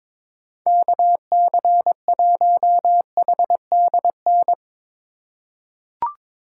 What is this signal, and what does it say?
Morse code, heard as a keyed tone, about 22 wpm, 700 Hz. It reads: KC1HDN
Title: USA Call Signs - 6 Characters - 3x Character Spacing - 22wpm